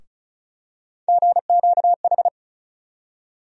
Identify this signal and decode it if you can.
Morse code, heard as a keyed tone, about 34 wpm, 700 Hz. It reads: GQH
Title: USA Call Sign Postfixes - 3 Letters 35wpm